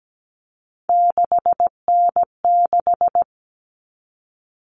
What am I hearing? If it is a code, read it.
Morse code, heard as a keyed tone, about 17 wpm, 700 Hz. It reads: 6N6
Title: All Cut Numbers - 3-Digit 17wpm